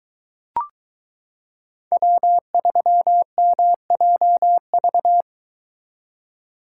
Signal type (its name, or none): Morse code